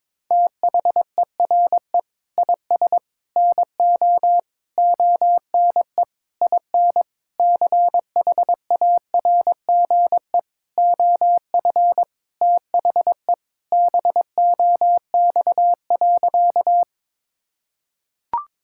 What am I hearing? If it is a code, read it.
Morse code, heard as a keyed tone, about 22 wpm, 700 Hz. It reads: THERE IS NO ONE IN CHARGE OF THE BOX.
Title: Sentences from Top 1000 Words - Review Part 6 - 22wpm